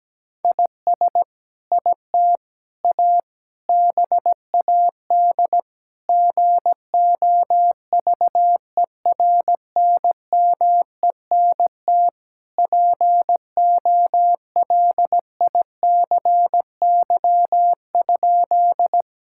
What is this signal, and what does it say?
Morse code, heard as a keyed tone, about 17 wpm, 700 Hz. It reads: IS IT A BAD GOVERNMENT POLICY?
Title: Sentences from Top 1000 Words - Review Part 5 - 17wpm